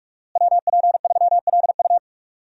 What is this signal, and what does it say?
Morse code, heard as a keyed tone, about 46 wpm, 700 Hz. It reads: WP3LU
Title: USA Call Signs - 5 Characters - Repeated 2x - 45wpm